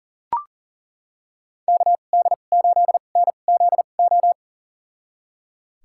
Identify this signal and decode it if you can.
Morse code, heard as a keyed tone, about 40 wpm, spaced as for 29 wpm, 700 Hz. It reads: KD8NZO
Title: USA Call Signs - 6 Characters - 2x Character Spacing - 40wpm